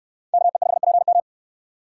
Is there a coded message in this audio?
Yes: Morse code from 0.3 to 1.2 s, heard as a keyed tone, about 69 wpm, 700 Hz. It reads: KE5PM